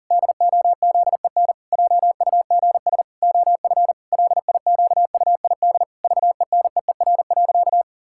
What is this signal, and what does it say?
Morse code, heard as a keyed tone, about 40 wpm, 700 Hz. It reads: DOZEN JUGS OF LIQUID VENEER.
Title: Pangram Challenge v2 - 40wpm